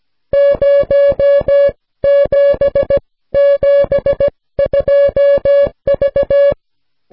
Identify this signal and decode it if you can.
Morse code, heard as a keyed tone, about 17 wpm, spaced as for 14 wpm, 560 Hz. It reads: 0772V